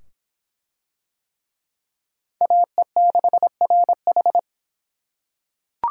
Morse code, heard as a keyed tone, about 24 wpm, 700 Hz. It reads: AE6RH